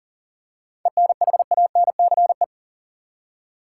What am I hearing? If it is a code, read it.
Morse code, heard as a keyed tone, about 40 wpm, 700 Hz. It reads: ENHANCE
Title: Seven Letter Words - 40wpm